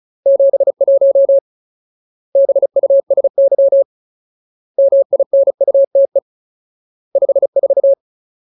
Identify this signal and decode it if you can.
Morse code, heard as a keyed tone, about 34 wpm, 550 Hz. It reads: 71 BUSY MINUTE 54
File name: Ultimate Word List / 35WPM Ultimate